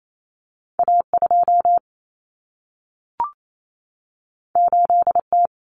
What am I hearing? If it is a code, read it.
Morse code, heard as a keyed tone, about 28 wpm, 700 Hz. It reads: A2 8T
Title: Common Cut Numbers - 2-Digit 28wpm